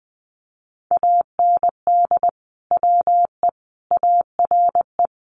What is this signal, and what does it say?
Morse code, heard as a keyed tone, about 20 wpm, 700 Hz. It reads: AND WE ARE
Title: Sets of 5 Words Encore2 20wpm